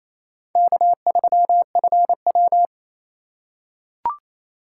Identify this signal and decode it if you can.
Morse code, heard as a keyed tone, about 28 wpm, 700 Hz. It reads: K3FW
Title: USA Call Signs - 4 Characters - 3x Character Spacing - 28wpm